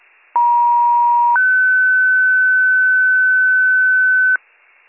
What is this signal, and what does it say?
Two-tone page, heard as a one-second tone, then a three-second tone, about 950 then 1500 Hz, with noise. It is heard from 0.3 to 4.4 s.